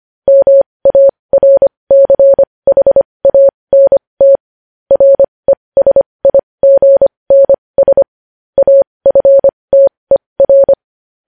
Morse code, heard as a keyed tone, about 25 wpm, spaced as for 20 wpm, 550 Hz. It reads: MARCHANT RESIGNS AFTER
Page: news headlines in morse code at 20 wpm